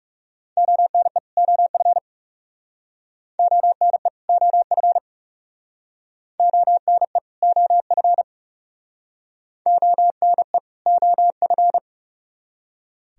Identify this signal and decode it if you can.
Morse code, heard as a keyed tone, about 38 wpm, 700 Hz. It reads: ONE OF ONE OF ONE OF ONE OF